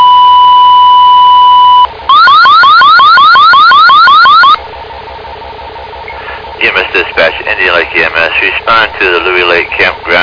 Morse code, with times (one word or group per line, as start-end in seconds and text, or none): none